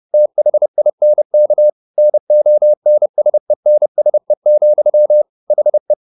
Morse code, heard as a keyed tone, about 30 wpm, 600 Hz. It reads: THINK NONSENSE, HE